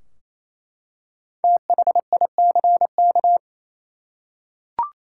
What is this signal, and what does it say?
Morse code, heard as a keyed tone, about 28 wpm, 700 Hz. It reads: THICK
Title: Five Letter Words - 28wpm